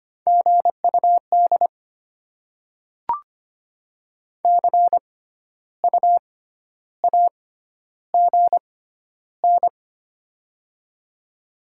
Morse code, heard as a keyed tone, about 25 wpm, spaced as for 12 wpm, 700 Hz. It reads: GUD CU AGN
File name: Top 20 QSO Elements - 6x Character Spacing - 25wpm